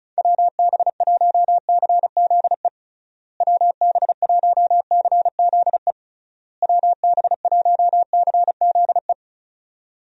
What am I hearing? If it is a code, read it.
Morse code, heard as a keyed tone, about 34 wpm, 700 Hz. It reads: WB1CZE WB1CZE WB1CZE
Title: USA Call Signs - 6 Characters - Repeated 3x - 35wpm